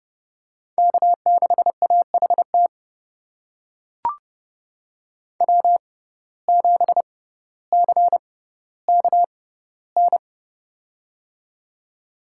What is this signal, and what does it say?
Morse code, heard as a keyed tone, about 30 wpm, spaced as for 15 wpm, 700 Hz. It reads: K6AHT W7CKN
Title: USA Call Signs - 5 Characters - 6x Character Spacing - 30wpm